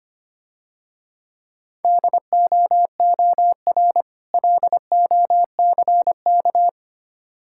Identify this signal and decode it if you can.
Morse code, heard as a keyed tone, about 26 wpm, 700 Hz. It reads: DOOR LOCK
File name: Easy Words - Speed-Racing with Hints - 25wpm